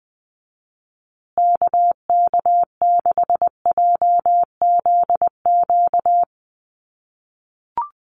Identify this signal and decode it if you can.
Morse code, heard as a keyed tone, about 20 wpm, 700 Hz. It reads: KK6JZQ
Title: USA Call Signs Repeated 2x 20wpm